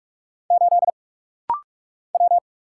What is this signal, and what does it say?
Morse code, heard as a keyed tone, about 45 wpm, 700 Hz. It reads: 8 W
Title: Single Letter-Number Alt - Rapid-Fire 45wpm